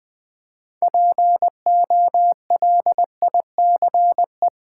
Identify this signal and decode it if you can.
Morse code, heard as a keyed tone, about 20 wpm, 700 Hz. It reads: POLICE
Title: Build A Word - 3 Letters At A Time - 9-Letter Words - 20wpm